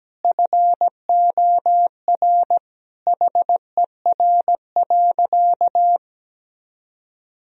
Morse code, heard as a keyed tone, about 17 wpm, 700 Hz. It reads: FOR HER.
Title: Sentences from Top 1000 Words - Review Part 3 - 17wpm